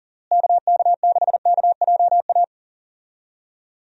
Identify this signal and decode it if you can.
Morse code, heard as a keyed tone, about 37 wpm, 700 Hz. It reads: KK6KJA
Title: USA Call Signs - 6 Characters - Repeated 3x - 40wpm